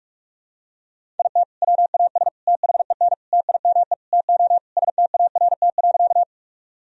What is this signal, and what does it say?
Morse code, heard as a keyed tone, about 46 wpm, 700 Hz. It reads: IT WAS THEN TIME TO START.
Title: Sentences from Top 1000 Words - Review Part 1 - 45wpm